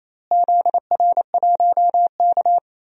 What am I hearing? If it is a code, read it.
Morse code, heard as a keyed tone, about 28 wpm, 700 Hz. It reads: ZR1K
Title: DXCC-100 Call Signs - 4 Characters - Repeated 2x - 28wpm